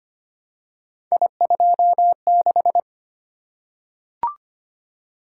Morse code, heard as a keyed tone, about 25 wpm, 700 Hz. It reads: I26
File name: Three Letter-Number Combinations - 25wpm